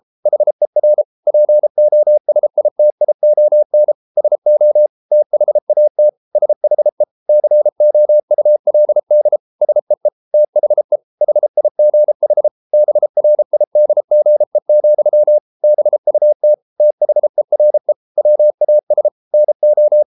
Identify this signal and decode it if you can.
Morse code, heard as a keyed tone, about 33 wpm, 600 Hz. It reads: HER POSITION SO THAT SHE COULD SEE THE HIGH BRIDGE, BUT THERE WAS NO